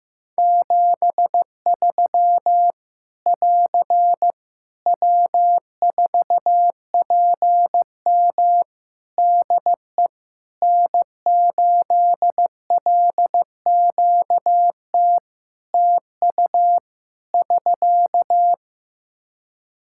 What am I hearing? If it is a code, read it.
Morse code, heard as a keyed tone, about 15 wpm, 700 Hz. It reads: 73 <AR> W4PM DE N8LQT TU <SK>
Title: Practice QSO - Exchange 4 - Ending - 15wpm